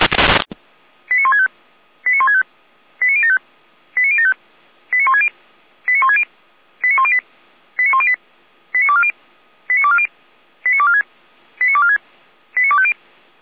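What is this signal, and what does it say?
Five-tone selective call (ZVEI), heard as quick runs of five tones, tones of about 80 ms. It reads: 89165 89165 89075 89075 89160 89160 89108 89108 89240 89240 89246 89246 89250
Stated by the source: ZVEI five-tone call received on SDR, feuerwehr-orlamuende.org (fire brigade Orlamünde).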